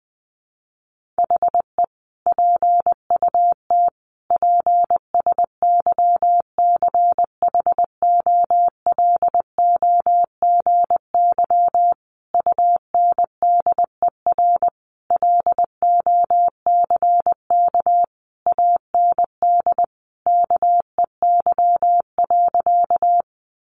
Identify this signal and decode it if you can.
Morse code, heard as a keyed tone, about 20 wpm, 700 Hz. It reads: HE PUT PSYCHOLOGY UNDER LOCK AND KEY.
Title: Sentences from Top 1400 Words 20wpm